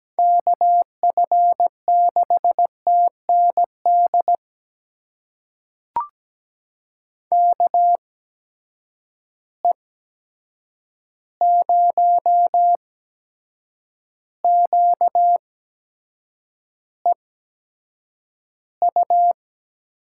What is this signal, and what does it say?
Morse code, heard as a keyed tone, about 17 wpm, spaced as for 7 wpm, 700 Hz. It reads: KF6TND KE0QEU
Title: USA Call Signs - 6 Characters - 8x Character Spacing - 17wpm